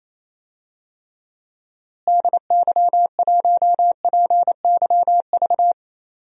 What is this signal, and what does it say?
Morse code, heard as a keyed tone, about 28 wpm, 700 Hz. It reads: DY1PYV
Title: DXCC-100 Call Signs - 6 Characters - Repeated 3x - 28wpm